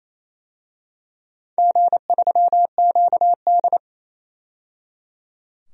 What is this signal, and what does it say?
Morse code, heard as a keyed tone, about 28 wpm, 690 Hz. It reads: G3QD
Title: DXCC-100 Call Signs - 4 Characters - 28wpm